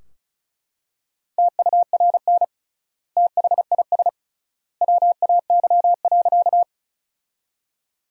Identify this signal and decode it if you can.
Morse code, heard as a keyed tone, about 34 wpm, 700 Hz. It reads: TURN THIS WAY.
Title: 3x Word-Spacing - Sentences from Top 200 Words - 35wpm